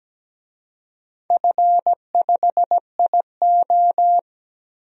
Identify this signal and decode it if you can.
Morse code, heard as a keyed tone, about 17 wpm, 700 Hz. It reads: F5IO
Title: DXCC-100 Call Signs - 4 Characters - Repeated 3x - 17wpm